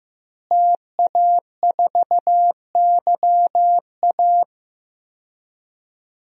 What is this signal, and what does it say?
Morse code, heard as a keyed tone, about 15 wpm, 700 Hz. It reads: TA4YA
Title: DXCC-100 Call Signs - 5 Characters - Repeated 3x - 15wpm